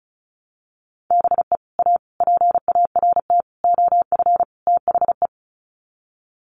Morse code, heard as a keyed tone, about 34 wpm, 700 Hz. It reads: BE A PART OF THE
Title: Sets of 5 Words Encore 35wpm